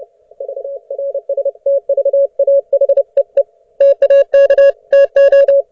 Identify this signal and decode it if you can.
Morse code, heard as a keyed tone, about 30 wpm, 560 Hz. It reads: VRSTVAHEE TAK TO